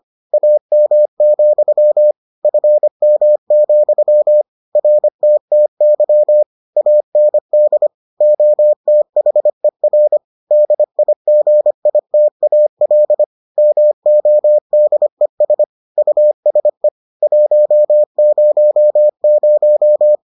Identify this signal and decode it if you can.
Morse code, heard as a keyed tone, about 25 wpm, 600 Hz. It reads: AM, FM, RTTY AND OTHER DIGITAL MODES USE 100